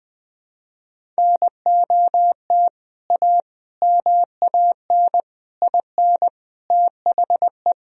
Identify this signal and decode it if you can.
Morse code, heard as a keyed tone, about 21 wpm, 700 Hz. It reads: NOT A MAN IN THE
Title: Sentences from Top 1000 Words - Review Part 1 - 20wpm